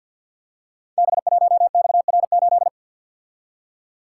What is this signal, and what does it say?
Morse code, heard as a keyed tone, about 53 wpm, 700 Hz. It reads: B1XR8
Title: Code Groups - 5 Letter:Numbers - 50wpm